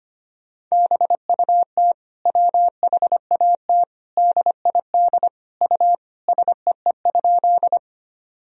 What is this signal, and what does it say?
Morse code, heard as a keyed tone, about 25 wpm, 700 Hz. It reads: BUT WHAT DID U SEE?
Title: Sentences from Top 100 Words and Abbreviations 25wpm